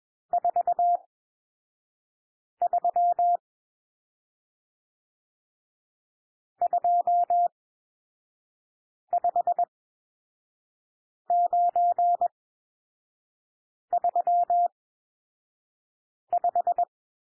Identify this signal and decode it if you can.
Morse code, heard as a keyed tone, about 21 wpm, spaced as for 5 wpm, 700 Hz. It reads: 43 25935
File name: cwmp3 / N1 SOUND NUMBERS 21 wpm